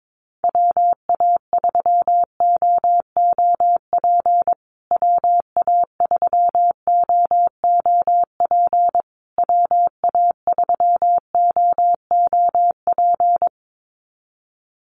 Morse code, heard as a keyed tone, about 22 wpm, 700 Hz. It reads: WA3OOP WA3OOP WA3OOP
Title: USA Call Signs Repeated 3x 22wpm